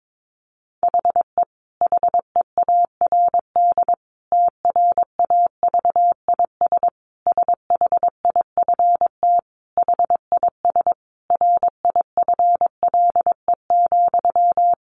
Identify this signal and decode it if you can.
Morse code, heard as a keyed tone, about 22 wpm, 700 Hz. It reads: HE HEARD TRAVIS SHIFT HIS RIFLE,
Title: A Sound Of Thunder 22wpm